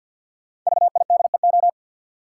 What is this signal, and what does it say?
Morse code, heard as a keyed tone, about 51 wpm, 700 Hz. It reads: VIDEO
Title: Sentences from Top 1000 Words - Review Part 4 - New Word Drill - 50wpm